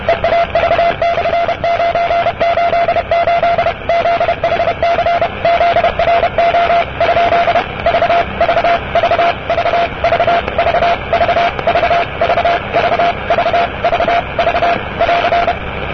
Morse code, heard as a keyed tone, about 32 wpm, 690 Hz, with radio noise. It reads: EAV/988 ZHC ZRO ? V V V V V V V V V V V V V <AR>
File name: europe / e-539-cw-eav988-rb-1973